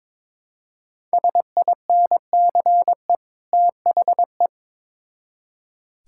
Morse code, heard as a keyed tone, about 22 wpm, 700 Hz. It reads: SINCE THE
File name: Build A Phrase - 2-Word Phrases - 22wpm